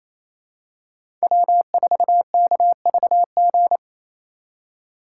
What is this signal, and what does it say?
Morse code, heard as a keyed tone, about 28 wpm, 700 Hz. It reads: W4KVG
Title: USA Call Signs - 2x Character Spacing - 28wpm